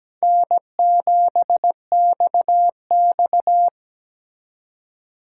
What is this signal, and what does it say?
Morse code, heard as a keyed tone, about 20 wpm, 700 Hz. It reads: N7XX
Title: Speed Racing - USA 1x2 Call Signs 17wpm